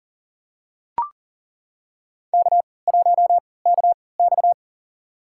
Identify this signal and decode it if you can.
Morse code, heard as a keyed tone, about 40 wpm, spaced as for 23 wpm, 700 Hz. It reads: K1KX
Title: USA Call Signs - 4 Characters - 3x Character Spacing - 40wpm